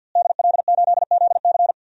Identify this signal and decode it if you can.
Morse code, heard as a keyed tone, about 51 wpm, 700 Hz. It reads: DL8ZC
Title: DXCC-100 Call Signs - 5 Characters - Repeated 3x - 50wpm